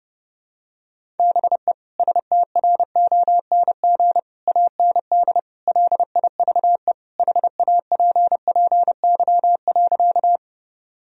Morse code, heard as a keyed tone, about 32 wpm, 700 Hz. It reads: BE STRONG AND LIVE HAPPY.
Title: Sentences from Top 1000 Words - Review Part 3 - 30wpm